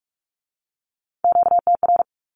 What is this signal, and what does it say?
Morse code, heard as a keyed tone, about 46 wpm, 700 Hz. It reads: QTR
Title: Q-Codes with Spoken Meaning - 45wpm